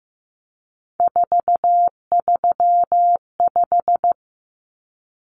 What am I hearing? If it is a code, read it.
Morse code, heard as a keyed tone, about 15 wpm, 700 Hz. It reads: 435
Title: Common Cut Numbers - 3-Digit 15wpm